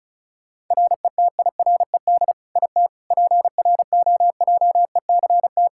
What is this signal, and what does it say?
Morse code, heard as a keyed tone, about 34 wpm, 700 Hz. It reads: RETIRED IT PROJECT
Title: Practice QSO - Exchange 3 - More Info - 35wpm